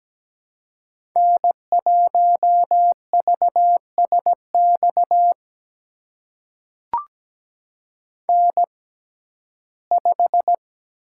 Morse code, heard as a keyed tone, about 17 wpm, spaced as for 12 wpm, 700 Hz. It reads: N1VSX N5
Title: USA Call Signs - 5 Characters - 6x Character Spacing - 17wpm